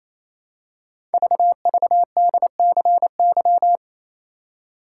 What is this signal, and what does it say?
Morse code, heard as a keyed tone, about 27 wpm, 700 Hz. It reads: VVDCY
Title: Code Groups - 5 Letters - 28wpm